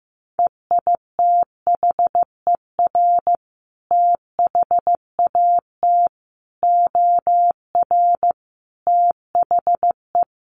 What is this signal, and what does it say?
Morse code, heard as a keyed tone, about 15 wpm, 700 Hz. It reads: EITHER THAT OR THE
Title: Sentences from Top 1400 Words 15wpm